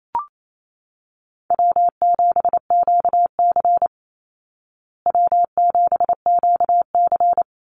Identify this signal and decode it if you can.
Morse code, heard as a keyed tone, about 28 wpm, 700 Hz. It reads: W7QC W7QC
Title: Easy Call Signs Repeated 3x - 28wpm